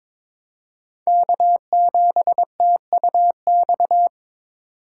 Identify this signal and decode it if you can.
Morse code, heard as a keyed tone, about 23 wpm, 700 Hz. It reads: K7TUX
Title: USA Call Signs - 5 Characters - 6x Character Spacing - 22wpm